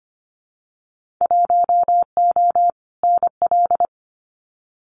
Morse code, heard as a keyed tone, about 25 wpm, 700 Hz. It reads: 1O NL
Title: Winter Field Day Exchanges Encore 25wpm